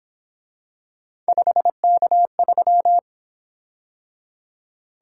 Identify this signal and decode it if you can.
Morse code, heard as a keyed tone, about 26 wpm, 700 Hz. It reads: 5K3
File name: Speed-Racing - Three Letter-Number Combinations - 22wpm